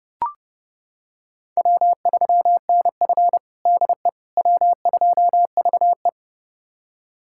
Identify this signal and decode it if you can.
Morse code, heard as a keyed tone, about 31 wpm, 700 Hz. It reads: W3NF DE W2VE
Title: USA Callsign DE Callsign - 4 Characters - 30wpm